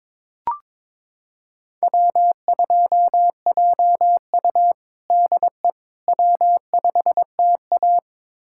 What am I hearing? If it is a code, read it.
Morse code, heard as a keyed tone, about 22 wpm, 700 Hz. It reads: W2JU DE W5TA
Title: USA Callsign DE Callsign - 4 Characters - 22wpm